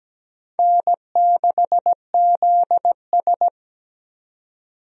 Morse code, heard as a keyed tone, about 17 wpm, 700 Hz. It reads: N6ZS
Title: USA Call Signs - 4 Characters - 17wpm